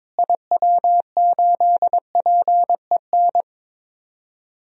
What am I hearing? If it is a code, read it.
Morse code, heard as a keyed tone, about 22 wpm, 700 Hz. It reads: IW8PEN
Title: DXCC-100 Call Signs - 6 Characters - Repeated 2x - 22wpm